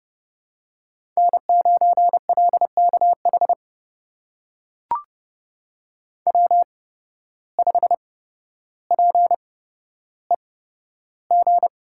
Morse code, heard as a keyed tone, about 30 wpm, spaced as for 13 wpm, 700 Hz. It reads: N9LKH W5PEG
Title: USA Call Signs - 5 Characters - 8x Character Spacing - 30wpm